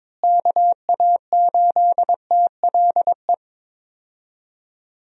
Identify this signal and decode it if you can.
Morse code, heard as a keyed tone, about 23 wpm, 700 Hz. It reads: KA8TLE
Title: USA Call Signs - 6 Characters - Repeated 2x - 22wpm